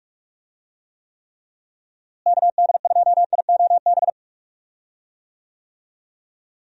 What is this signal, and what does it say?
Morse code, heard as a keyed tone, about 45 wpm, 700 Hz. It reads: KD2IOB
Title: USA Call Signs - 6 Characters - Repeated 3x - 45wpm